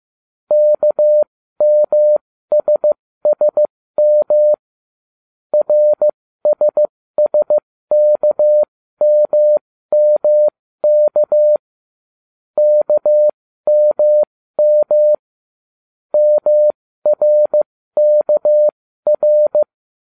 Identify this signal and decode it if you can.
Morse code, heard as a keyed tone, about 15 wpm, spaced as for 13 wpm, 600 Hz. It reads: KMSSM RSSKMMK KMM MRKR